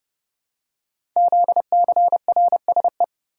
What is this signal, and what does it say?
Morse code, heard as a keyed tone, about 30 wpm, 700 Hz. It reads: ZCRSE
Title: Code Groups - 5 Letters - 30wpm